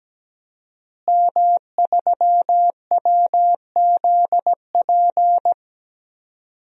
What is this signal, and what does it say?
Morse code, heard as a keyed tone, about 17 wpm, 700 Hz. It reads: M3WZP
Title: DXCC-100 Call Signs - 5 Characters - Repeated 2x - 17wpm